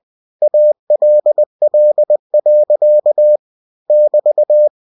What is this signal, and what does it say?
Morse code, heard as a keyed tone, about 20 wpm, 600 Hz. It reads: ALL. <BT>
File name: cw-books / winnie-the-pooh / winnie-20wpm-page-0025